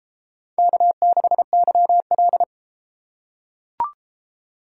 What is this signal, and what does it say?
Morse code, heard as a keyed tone, about 33 wpm, 700 Hz. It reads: K6YL